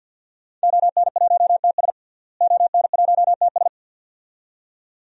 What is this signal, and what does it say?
Morse code, heard as a keyed tone, about 46 wpm, 700 Hz. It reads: ON1TS ON1TS
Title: DXCC-100 Call Signs - 5 Characters - Repeated 2x - 50wpm